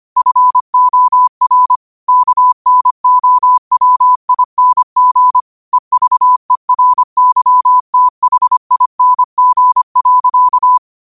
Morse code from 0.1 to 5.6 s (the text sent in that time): FOR KNOWING